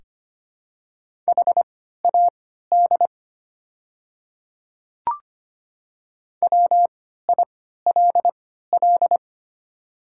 Morse code, heard as a keyed tone, about 25 wpm, spaced as for 14 wpm, 700 Hz. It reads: HAD WILL
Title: 3x Character Spacing - Top 100 Words - 25wpm